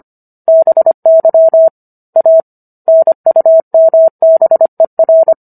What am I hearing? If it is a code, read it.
Morse code, heard as a keyed tone, about 25 wpm, 650 Hz. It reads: BY A NUMBER